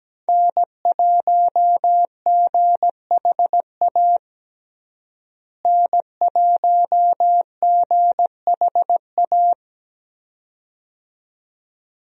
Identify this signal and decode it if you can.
Morse code, heard as a keyed tone, about 17 wpm, 700 Hz. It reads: N1GHA N1GHA
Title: USA Call Signs - 5 Characters - Repeated 2x - 17wpm